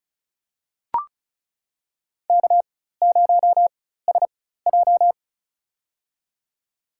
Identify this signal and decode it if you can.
Morse code, heard as a keyed tone, about 34 wpm, spaced as for 16 wpm, 700 Hz. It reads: K0SJ
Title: USA Call Signs - 4 Characters - 4x Character Spacing - 35wpm